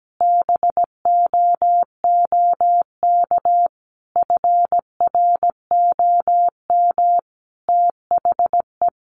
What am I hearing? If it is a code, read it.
Morse code, heard as a keyed tone, about 17 wpm, 700 Hz. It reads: BOOK FROM THE
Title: Sentences from Top 1300 Words 17wpm